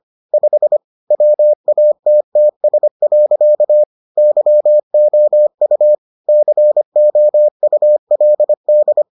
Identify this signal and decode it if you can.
Morse code, heard as a keyed tone, about 25 wpm, 600 Hz. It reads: 5 WATTS. YOU COULD